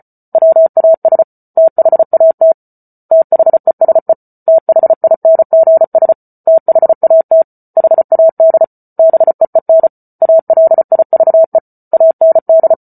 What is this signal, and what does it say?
Morse code, heard as a keyed tone, about 34 wpm, 650 Hz. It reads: WAS THAT THESE THINGS THAT HAD BEEN ALIVE AND